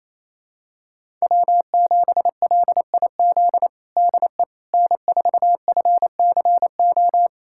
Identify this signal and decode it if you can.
Morse code, heard as a keyed tone, about 28 wpm, 700 Hz. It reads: W7LIZ DE N4FCO
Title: USA Callsign DE Callsign - 5 Characters - 28wpm